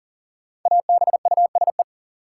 Morse code, heard as a keyed tone, about 40 wpm, 700 Hz. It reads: ABUSE
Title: Sentences from Top 1300 Words - New Word Drill Alt - 40wpm